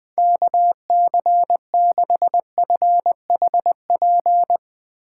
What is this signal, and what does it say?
Morse code, heard as a keyed tone, about 21 wpm, 700 Hz. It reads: KC6FHP